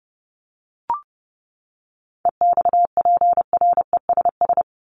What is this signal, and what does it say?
Morse code, heard as a keyed tone, about 30 wpm, 700 Hz. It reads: EXPRESS
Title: Top 1001-1250 Words 30wpm